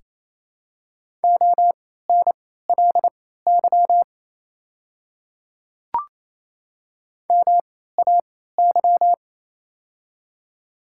Morse code, heard as a keyed tone, about 28 wpm, spaced as for 16 wpm, 700 Hz. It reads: ONLY MAY
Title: 3x Character Spacing - Top 200 Words - 28wpm